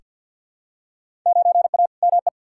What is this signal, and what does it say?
Morse code, heard as a keyed tone, about 46 wpm, 700 Hz. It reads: 9A ME
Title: Field Day Exchanges Encore - 50wpm